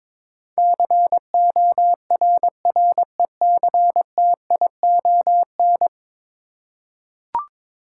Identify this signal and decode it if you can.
Morse code, heard as a keyed tone, about 22 wpm, 700 Hz. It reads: CORRECTION
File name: practice / 22wpm / 10 Letter Words - First 3 Letters Given - 22wpm